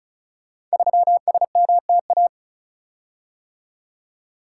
Morse code, heard as a keyed tone, about 36 wpm, 700 Hz. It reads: 3SMTA
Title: Code Groups - 5 Letter:Numbers - 35wpm